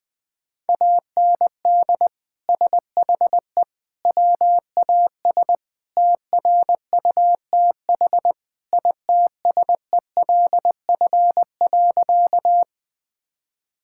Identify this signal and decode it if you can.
Morse code, heard as a keyed tone, about 20 wpm, 700 Hz. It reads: AND SHE WAS TRUTH ITSELF.
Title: Sentences from Top 1000 Words - Review Part 5 - 20wpm